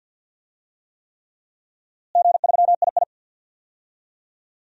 Morse code, heard as a keyed tone, about 51 wpm, 700 Hz. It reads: G3II
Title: DXCC-100 Call Signs - 4 Characters - Repeated 3x - 50wpm